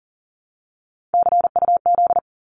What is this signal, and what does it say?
Morse code, heard as a keyed tone, about 40 wpm, 700 Hz. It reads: CUZ
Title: Abbreviations 40wpm